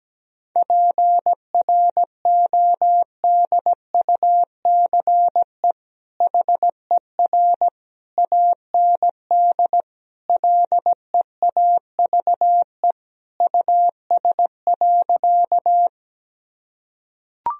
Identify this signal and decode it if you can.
Morse code, heard as a keyed tone, about 17 wpm, 700 Hz. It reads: PRODUCE HER AND LEAVE US.
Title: Sentences from Top 1000 Words - Review Part 3 - 17wpm